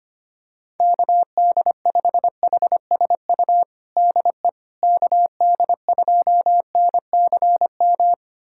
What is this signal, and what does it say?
Morse code, heard as a keyed tone, about 25 wpm, 700 Hz. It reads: KD5HSU DE KD2NCM
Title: USA Callsign DE Callsign - 6 Characters - 25wpm